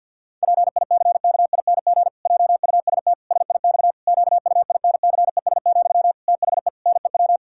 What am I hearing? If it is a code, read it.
Morse code, heard as a keyed tone, about 50 wpm, 700 Hz. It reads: PICKING JUST SIX QUINCES, THE NEW
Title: Pangram Challenge v2 - 50wpm